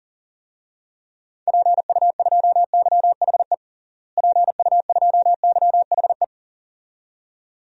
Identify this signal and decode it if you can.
Morse code, heard as a keyed tone, about 40 wpm, 700 Hz. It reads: PU2YHE PU2YHE
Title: DXCC-100 Call Signs - 6 Characters - Repeated 2x - 40wpm